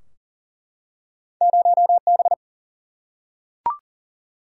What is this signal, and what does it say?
Morse code, heard as a keyed tone, about 42 wpm, 700 Hz. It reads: 0B
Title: Single-Letter Single-Number 40wpm